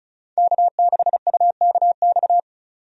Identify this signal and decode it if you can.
Morse code, heard as a keyed tone, about 36 wpm, 700 Hz. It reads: K6UKX
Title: USA Call Signs - 5 Characters - 35wpm